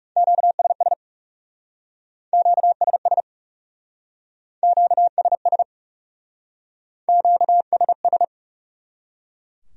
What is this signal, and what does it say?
Morse code, heard as a keyed tone, about 39 wpm, 700 Hz. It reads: QSS QSS QSS QSS